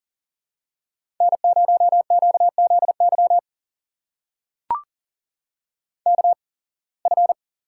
Morse code, heard as a keyed tone, about 43 wpm, spaced as for 27 wpm, 700 Hz. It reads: N0QZY KF
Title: USA Call Signs - 5 Characters - 8x Character Spacing - 40wpm